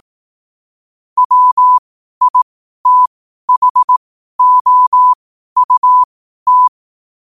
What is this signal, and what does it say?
Morse code, heard as a keyed tone, about 18 wpm, spaced as for 12 wpm, 1000 Hz. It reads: WITHOUT
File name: qotdfiles / 12wpm